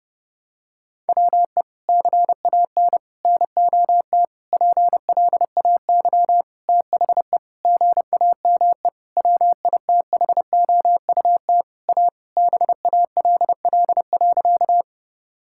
Morse code, heard as a keyed tone, about 30 wpm, 700 Hz. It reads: WE CAN NOT PLAY THE GAME WITHOUT A BALL.
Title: Sentences from Top 1000 Words - Review Part 6 - 30wpm